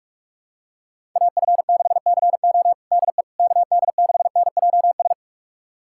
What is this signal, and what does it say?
Morse code, heard as a keyed tone, about 46 wpm, 700 Hz. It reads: AF6CO DE KD6NJS
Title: USA Callsign DE Callsign 45wpm